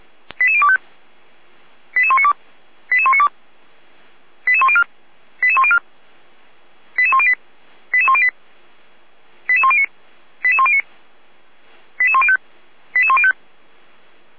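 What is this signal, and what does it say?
Five-tone selective call (ZVEI), heard as quick runs of five tones, tones of about 70 ms. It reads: 88025 88192 88192 88104 88104 88108 88108 88109 88109 88105 88105
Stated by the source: Five-tone ZVEI selective call received on SDR, fire brigade Feuerwehr Orlamünde.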